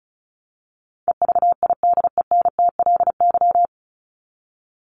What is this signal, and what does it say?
Morse code, heard as a keyed tone, about 34 wpm, 700 Hz. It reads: EVIDENTLY
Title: LY Suffix - Top 100 Words with Base Words 35wpm